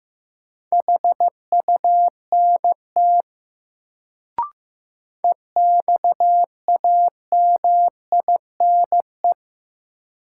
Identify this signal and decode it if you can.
Morse code, heard as a keyed tone, about 15 wpm, 700 Hz. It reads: HUNT EXAMINE
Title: Sentences from Top 1600 Words - New Word Drill Alt - 15wpm